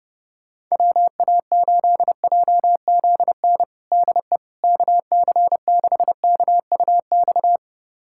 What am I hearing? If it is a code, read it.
Morse code, heard as a keyed tone, about 32 wpm, 700 Hz. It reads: WA8JZN DE KC6KUX
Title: USA Callsign DE Callsign - 6 Characters - 30wpm